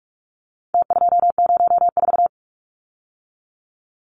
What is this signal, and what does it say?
Morse code, heard as a keyed tone, about 44 wpm, 700 Hz. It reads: T204
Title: Numbers - Contest Style - 4-Digit 45wpm